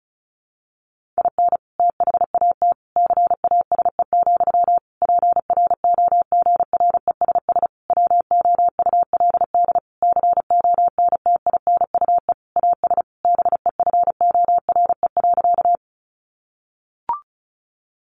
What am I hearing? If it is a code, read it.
Morse code, heard as a keyed tone, about 34 wpm, 700 Hz. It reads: IN THAT CASE, PROGRESS WOULD CONTINUE AS BEFORE.
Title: Sentences from Top 1400 Words 35wpm